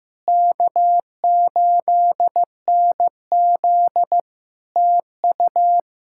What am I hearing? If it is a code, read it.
Morse code, heard as a keyed tone, about 15 wpm, 700 Hz. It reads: K8NZ TU
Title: Practice QSO - Exchange 4 - Ending - 15wpm